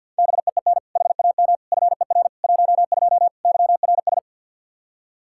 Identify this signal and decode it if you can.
Morse code, heard as a keyed tone, about 47 wpm, 700 Hz. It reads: BEEN HAM FER 12 YRS
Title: Practice QSO - Exchange 3 - More Info - 50wpm